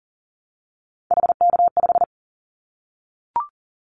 Morse code, heard as a keyed tone, about 40 wpm, 700 Hz. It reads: HK5
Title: Callsign Prefixes for Top 100 DXCC - 40wpm